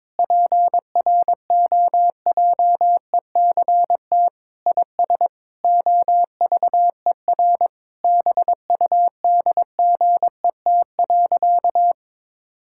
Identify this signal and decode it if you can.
Morse code, heard as a keyed tone, about 22 wpm, 700 Hz. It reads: PROJECT IS OVER BUDGET.
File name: practice / 22wpm / Sentences with 8-Word Phrases - 22wpm